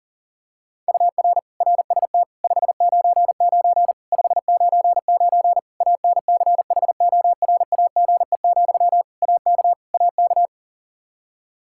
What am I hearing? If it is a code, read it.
Morse code, heard as a keyed tone, about 40 wpm, 700 Hz. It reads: UR RST 599 599 ANCHORAGE, AK AK
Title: Practice QSO - Exchange 1 - Beginning - 40wpm